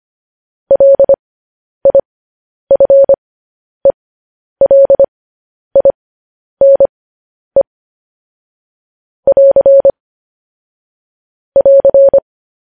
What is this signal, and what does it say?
Morse code, heard as a keyed tone, about 25 wpm, spaced as for 10 wpm, 550 Hz. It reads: LIFELINE <AR> <AR>
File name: podcast / News headlines-1x-25x10